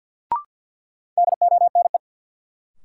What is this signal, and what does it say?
Morse code, heard as a keyed tone, about 52 wpm, 700 Hz. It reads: DONE
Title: Sentences from Top 500 Words - New Word Drill - 50wpm